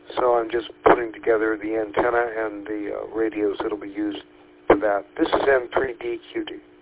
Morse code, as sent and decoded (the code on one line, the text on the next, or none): none